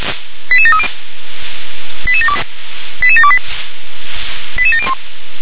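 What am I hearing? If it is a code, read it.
Five-tone selective call (ZVEI), heard as quick runs of five tones, tones of about 70 ms. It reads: 88730 88728 88701
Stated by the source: ZVEI five-tone call received on an SDR.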